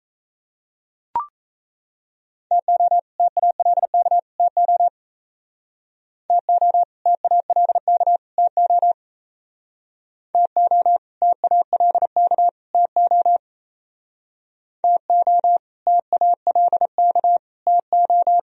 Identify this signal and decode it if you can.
Morse code, heard as a keyed tone, about 36 wpm, 700 Hz. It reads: TO TALK TO TO TALK TO TO TALK TO TO TALK TO